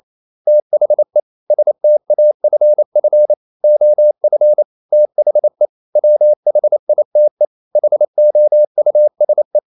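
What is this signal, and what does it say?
Morse code, heard as a keyed tone, about 28 wpm, 600 Hz. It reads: THE STAFF OF THE WHITE HOUSE
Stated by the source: Morse code practice at 28 wpm, text of Ken Follett's novel Never in English, from CW med Gnister.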